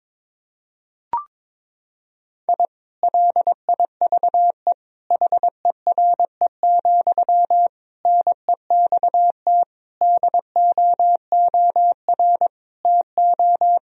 Morse code, heard as a keyed tone, about 22 wpm, 700 Hz. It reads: I LIVE HERE, NEXT DOOR TO
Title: Sentences from Top 1000 Words - Review Part 1 - 22wpm